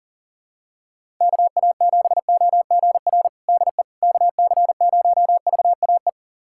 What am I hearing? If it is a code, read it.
Morse code, heard as a keyed tone, about 40 wpm, 700 Hz. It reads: KA7OGR DE KC0VAE
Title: USA Callsign DE Callsign - 6 Characters - 40wpm